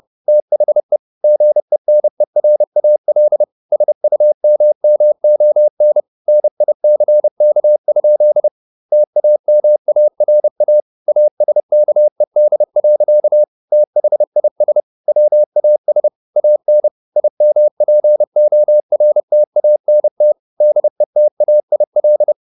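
Morse code, heard as a keyed tone, about 30 wpm, 600 Hz. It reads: THE GENERAL SUMMON NICK? TAMARA ASKED. THIS WAS AN IMPORTANT DETAIL
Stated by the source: Morse code at 30 wpm, the novel Never by Ken Follett in English, from CW med Gnister.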